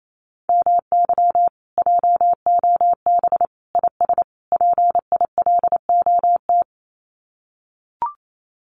Morse code, heard as a keyed tone, about 28 wpm, 700 Hz. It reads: MY JOB IS PILOT
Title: Common QSO Phrases Encore 28wpm